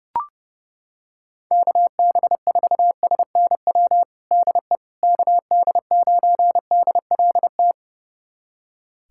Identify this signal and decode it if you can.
Morse code, heard as a keyed tone, about 30 wpm, 700 Hz. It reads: KB4SNW DE KD9DLT
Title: USA Callsign DE Callsign 30wpm